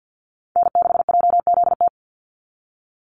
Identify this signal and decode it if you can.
Morse code, heard as a keyed tone, about 50 wpm, 700 Hz. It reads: N6JZT
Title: NA SOTA Activator Callsigns 50wpm